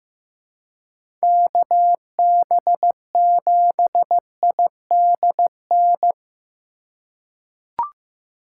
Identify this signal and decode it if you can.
Morse code, heard as a keyed tone, about 15 wpm, 700 Hz. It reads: KB7IDN
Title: USA Call Signs - 6 Characters - 2x Character Spacing - 15wpm